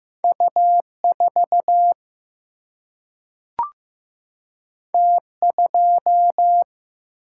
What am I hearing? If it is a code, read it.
Morse code, heard as a keyed tone, about 15 wpm, 700 Hz. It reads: U 4 T 2
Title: Single-Letter Single-Number 15wpm